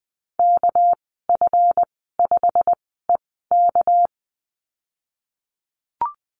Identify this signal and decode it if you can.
Morse code, heard as a keyed tone, about 20 wpm, spaced as for 15 wpm, 700 Hz. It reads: KF5EK
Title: 238 Lesson 38g - Letter q - Call Signs 20wpm